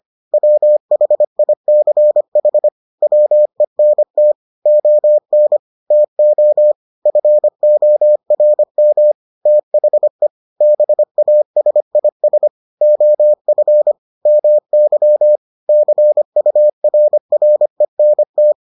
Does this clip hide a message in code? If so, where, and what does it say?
Yes: Morse code from 0.3 to 18.5 s, heard as a keyed tone, about 25 wpm, 600 Hz. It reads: WHICH WENT ON TO FORM THE BASIS OF MY CURRENT